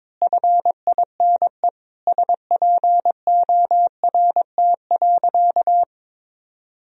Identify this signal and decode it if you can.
Morse code, heard as a keyed tone, about 21 wpm, 700 Hz. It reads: FINE SPORT.
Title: Sentences from Top 1000 Words - Review Part 4 - 22wpm